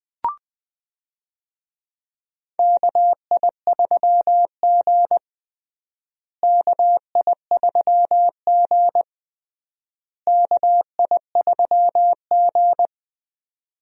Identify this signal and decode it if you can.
Morse code, heard as a keyed tone, about 20 wpm, 700 Hz. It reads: KI3G KI3G KI3G
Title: USA Call Signs - 4 Characters - Repeated 3x - 20wpm